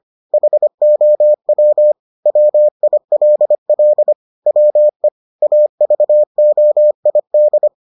Morse code, heard as a keyed tone, about 25 wpm, 600 Hz. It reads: HOW WILL WE AVOID